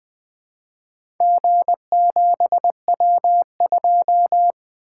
Morse code, heard as a keyed tone, about 20 wpm, 700 Hz. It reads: G7W2
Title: Four Letter-Number Combinations - 20wpm